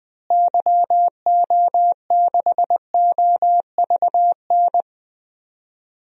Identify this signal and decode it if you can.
Morse code, heard as a keyed tone, about 21 wpm, 700 Hz. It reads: YO6OVN
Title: DXCC-100 Call Signs - 6 Characters - Repeated 2x - 20wpm